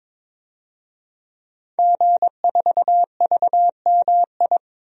Morse code, heard as a keyed tone, about 23 wpm, 700 Hz. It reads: G4VMI
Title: DXCC-100 Call Signs - 5 Characters - 22wpm